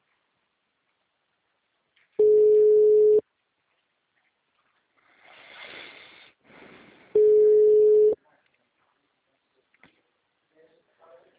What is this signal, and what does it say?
Ringback tone (Europe, 425 Hz), with speech.